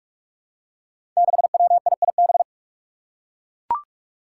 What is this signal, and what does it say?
Morse code, heard as a keyed tone, about 44 wpm, 700 Hz. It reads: 6WIIB
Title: Five Challenging Letter-Number Combinations 45wpm